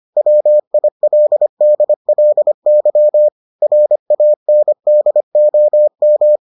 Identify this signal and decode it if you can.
Morse code, heard as a keyed tone, about 25 wpm, 600 Hz. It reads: WILDLY RANDOM